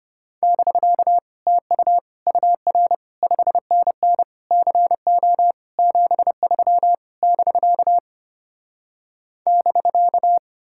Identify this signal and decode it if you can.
Morse code, heard as a keyed tone, about 33 wpm, 700 Hz. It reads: <BK> TU UR 5NN CO 73 <BK> <BK>